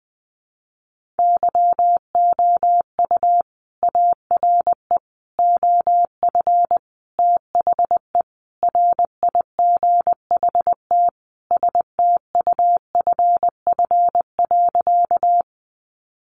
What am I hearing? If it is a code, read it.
Morse code, heard as a keyed tone, about 20 wpm, 700 Hz. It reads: YOU ARE OF THE RIGHT STUFF.
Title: Sentences from Top 1400 Words 20wpm